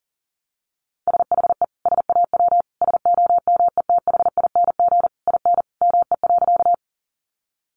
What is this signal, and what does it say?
Morse code, heard as a keyed tone, about 40 wpm, 700 Hz. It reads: SHE SAW SOMETHING IN ME.
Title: Sentences from Top 600 Words 40wpm